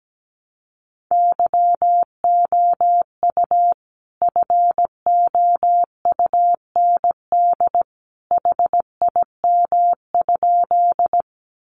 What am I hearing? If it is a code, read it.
Morse code, heard as a keyed tone, about 17 wpm, 700 Hz. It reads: YOU FOUND HIM?
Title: Sentences from Top 300 Words 17wpm